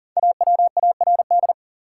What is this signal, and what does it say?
Morse code, heard as a keyed tone, about 40 wpm, 700 Hz. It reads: AWARD
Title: Five Letter Words - 40wpm